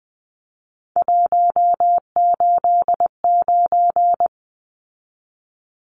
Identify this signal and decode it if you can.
Morse code, heard as a keyed tone, about 19 wpm, 700 Hz. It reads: 189
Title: Numbers - 3-Digit 20wpm